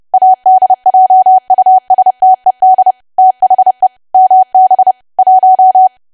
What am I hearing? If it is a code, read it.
Morse code, heard as a keyed tone, about 30 wpm, 750 Hz. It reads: ADJUSTED THE MB 1
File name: cwa / W1AW-Level3 / 30WPM